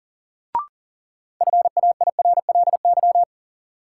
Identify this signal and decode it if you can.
Morse code, heard as a keyed tone, about 40 wpm, 700 Hz. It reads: FAIRLY
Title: Sentences from Top 1100 Words - New Word Drill - 40wpm